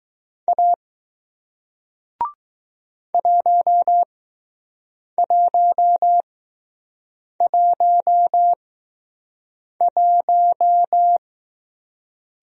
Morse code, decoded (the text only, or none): A 1111